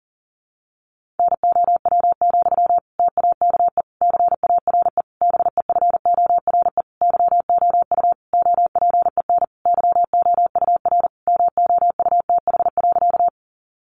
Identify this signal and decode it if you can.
Morse code, heard as a keyed tone, about 37 wpm, 700 Hz. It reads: NOW, TAKE CARE BEFORE YOU OPEN YOUR MOUTH.
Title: Sentences from Top 1500 Words 40wpm